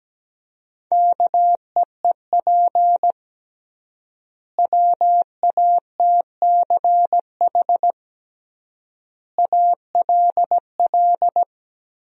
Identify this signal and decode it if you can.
Morse code, heard as a keyed tone, about 17 wpm, 700 Hz. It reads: KEEP WATCH ALL
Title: 3x Word-Spacing - Sentences from Top 300 Words - 17wpm